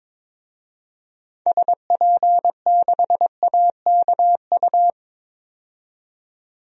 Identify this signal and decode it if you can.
Morse code, heard as a keyed tone, about 22 wpm, 700 Hz. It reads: SP6AKU
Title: DXCC-100 Call Signs - 6 Characters - 22wpm